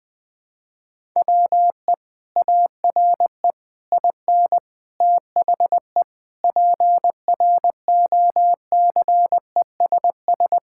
Morse code, heard as a keyed tone, about 19 wpm, 700 Hz. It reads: WE ARE IN THE PROCESS
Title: Sets of 6 Words - 20wpm